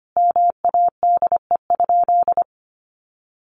Morse code, heard as a keyed tone, about 25 wpm, 700 Hz. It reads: MADE?
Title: Sentences from Top 1400 Words 25wpm